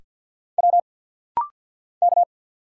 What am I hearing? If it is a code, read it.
Morse code, heard as a keyed tone, about 50 wpm, 690 Hz. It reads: W K
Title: Single Letter-Number Alt - Rapid-Fire 50wpm